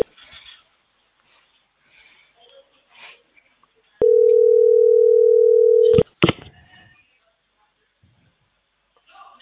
Ringback tone (North America), 440 + 480 Hz.